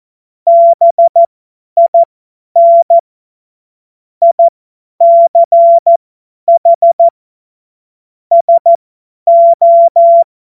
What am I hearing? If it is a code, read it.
Morse code, heard as a keyed tone, about 14 wpm, spaced as for 10 wpm, 680 Hz. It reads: BIN ICH SO
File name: Morsekurs / Witze / Witz 10 Juni 2025 50 F70